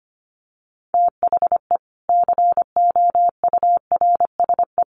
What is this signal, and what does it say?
Morse code, heard as a keyed tone, about 25 wpm, 700 Hz. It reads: THE COURSE
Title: Sets of 4 Words 25wpm